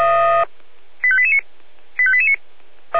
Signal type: five-tone selective call